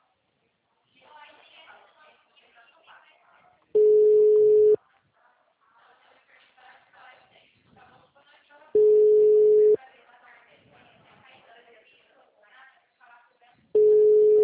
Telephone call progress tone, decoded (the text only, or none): ringback tone